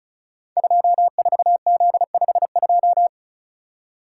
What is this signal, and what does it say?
Morse code, heard as a keyed tone, about 36 wpm, 700 Hz. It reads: 24Z52
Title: Code Groups - 5 Letter:Numbers - 35wpm